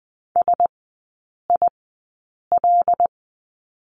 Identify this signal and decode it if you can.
Morse code, heard as a keyed tone, about 20 wpm, spaced as for 8 wpm, 700 Hz. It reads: SIL
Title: 074 Lesson 13e - Letter L - Three Characters 20wpm